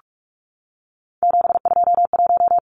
Morse code, heard as a keyed tone, about 44 wpm, 700 Hz. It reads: 721
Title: Numbers - 3-Digit 45wpm